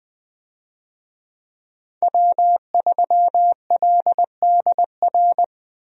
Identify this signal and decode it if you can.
Morse code, heard as a keyed tone, about 21 wpm, 700 Hz. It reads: W3LDR
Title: USA Call Signs - 5 Characters - Repeated 3x - 20wpm